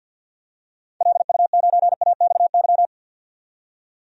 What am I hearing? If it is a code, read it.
Morse code, heard as a keyed tone, about 51 wpm, 700 Hz. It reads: LU9AXY
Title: DXCC-100 Call Signs - 6 Characters - 50wpm